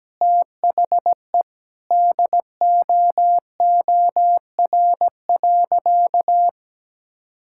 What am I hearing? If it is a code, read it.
Morse code, heard as a keyed tone, about 17 wpm, 700 Hz. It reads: THE DOOR.
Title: Sentences from Top 1000 Words - Review Part 6 - 17wpm